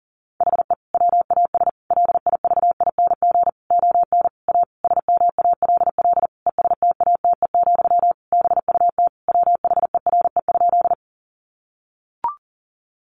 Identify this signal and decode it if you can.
Morse code, heard as a keyed tone, about 40 wpm, 700 Hz. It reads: HE WAS LIVING ON A SMALL ESTATE, BUT WHERE?
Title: Sentences from Top 1200 Words 40wpm